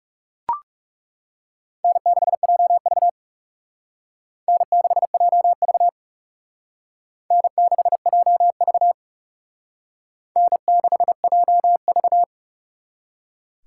Morse code, heard as a keyed tone, about 37 wpm, 700 Hz. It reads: N6JV N6JV N6JV N6JV
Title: Speed Racing - USA 1x2 Call Signs 30wpm